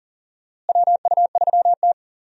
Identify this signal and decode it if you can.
Morse code, heard as a keyed tone, about 40 wpm, 700 Hz. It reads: WU3T
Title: USA Call Signs - 4 Characters - Repeated 2x - 40wpm